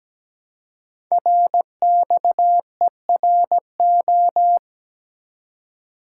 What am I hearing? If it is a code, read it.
Morse code, heard as a keyed tone, about 17 wpm, 700 Hz. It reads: RXERO
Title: Code Groups - 5 Letters - 17wpm